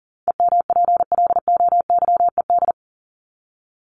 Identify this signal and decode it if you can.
Morse code, heard as a keyed tone, about 40 wpm, 700 Hz. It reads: EMPLOYED
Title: ED Suffix - Top 100 Words with Base Words 40wpm